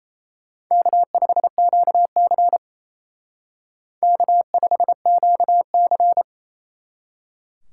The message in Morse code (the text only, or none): K5QC K5QC